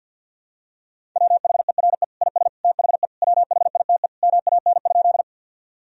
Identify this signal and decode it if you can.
Morse code, heard as a keyed tone, about 48 wpm, 700 Hz. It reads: WHERE IS THE WHITE MAN?
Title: Sentences from Top 1000 Words - Review Part 1 - 50wpm